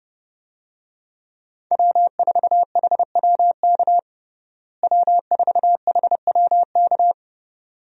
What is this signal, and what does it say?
Morse code, heard as a keyed tone, about 32 wpm, 700 Hz. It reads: W4HWK W4HWK
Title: USA Call Signs - 5 Characters - Repeated 2x - 30wpm